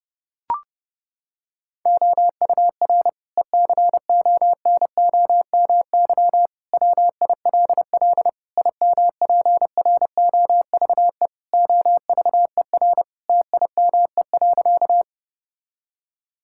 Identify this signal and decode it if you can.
Morse code, heard as a keyed tone, about 32 wpm, 700 Hz. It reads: OUR ECONOMY WILL IMPROVE OVER TIME.
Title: Sentences from Top 1000 Words - Review Part 5 - 30wpm